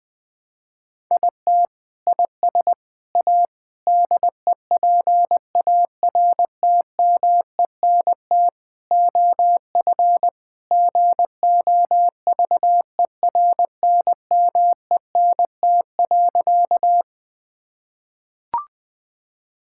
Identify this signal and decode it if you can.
Morse code, heard as a keyed tone, about 21 wpm, 700 Hz. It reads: IT IS A DEPARTMENT OF GOVERNMENT.
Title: Sentences from Top 1000 Words - Review Part 5 - 20wpm